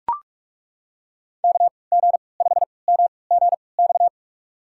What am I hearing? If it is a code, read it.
Morse code, heard as a keyed tone, about 41 wpm, spaced as for 25 wpm, 700 Hz. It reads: KG5MGX
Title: USA Call Signs - 3x Character Spacing - 45wpm